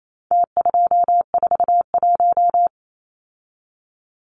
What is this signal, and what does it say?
Morse code, heard as a keyed tone, about 28 wpm, 700 Hz. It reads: T241
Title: Numbers - Contest Style - 4-Digit 28wpm